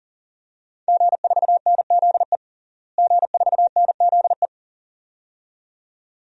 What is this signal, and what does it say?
Morse code, heard as a keyed tone, about 40 wpm, 700 Hz. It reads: G4NZE G4NZE
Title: DXCC-100 Call Signs - 5 Characters - Repeated 2x - 40wpm